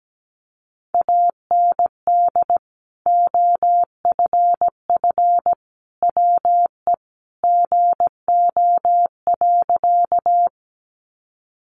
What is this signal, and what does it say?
Morse code, heard as a keyed tone, about 17 wpm, 700 Hz. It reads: AND OFF WE GO.
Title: Sentences from Top 200 Words 17wpm